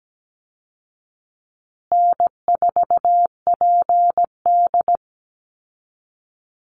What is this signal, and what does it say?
Morse code, heard as a keyed tone, about 17 wpm, 700 Hz. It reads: N4PD
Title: Call Signs Repeated 2x - 17wpm